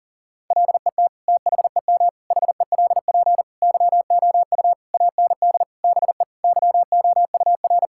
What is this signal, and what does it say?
Morse code, heard as a keyed tone, about 40 wpm, 700 Hz. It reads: LET THEM HELP YOU AND BE YOUR
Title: Sentences from Top 1600 Words 40wpm